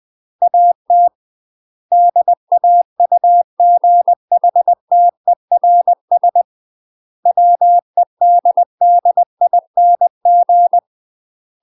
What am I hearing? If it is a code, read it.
Morse code, heard as a keyed tone, about 20 wpm, 700 Hz. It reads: AT DAUGHTERS WEDDING
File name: UK MASTER QSOs TRAINING / 20WPM / UK MASTER QSOs 0021